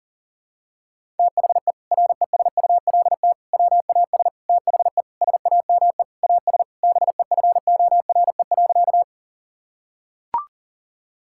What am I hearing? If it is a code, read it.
Morse code, heard as a keyed tone, about 37 wpm, 700 Hz. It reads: THE RESULT WAS THE SAME AS BEFORE.
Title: Sentences from Top 1000 Words - Review Part 5 - 40wpm